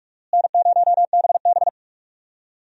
Morse code, heard as a keyed tone, about 46 wpm, 700 Hz. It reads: N0BB
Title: USA Call Signs - 4 Characters - Repeated 3x - 45wpm